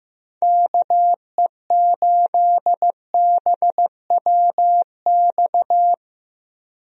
Morse code, heard as a keyed tone, about 15 wpm, 700 Hz. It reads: KE8BWX
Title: USA Call Signs - 6 Characters - Repeated 3x - 15wpm